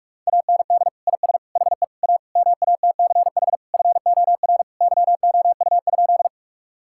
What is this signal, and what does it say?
Morse code, heard as a keyed tone, about 46 wpm, 700 Hz. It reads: AND IS HE A MATCH FOR YOU?
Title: Sentences from Top 1600 Words 45wpm